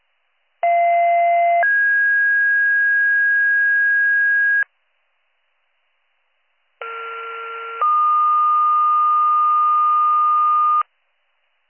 Two-tone page, heard as a one-second tone, then a three-second tone, about 690 then 1650 Hz, about 1370 then 1220 Hz. It is heard from 0.6 to 10.8 s.